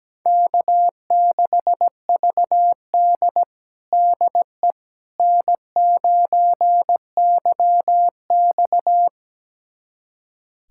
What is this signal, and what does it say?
Morse code, heard as a keyed tone, about 17 wpm, 700 Hz. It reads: K6VD DE N9YX
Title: USA Callsign DE Callsign - 4 Characters - 17wpm